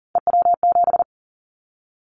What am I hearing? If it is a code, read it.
Morse code, heard as a keyed tone, about 41 wpm, 700 Hz. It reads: EW7
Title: Callsign Prefixes for Top 50 DXCC - 40wpm